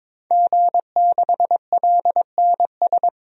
Morse code, heard as a keyed tone, about 22 wpm, 700 Hz. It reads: G6LNS
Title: DXCC-100 Call Signs - 5 Characters - 22wpm